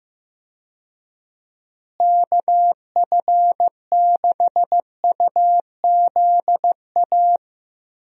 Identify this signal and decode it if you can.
Morse code, heard as a keyed tone, about 16 wpm, 700 Hz. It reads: KF6UZA
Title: USA Call Signs - 6 Characters - Repeated 3x - 15wpm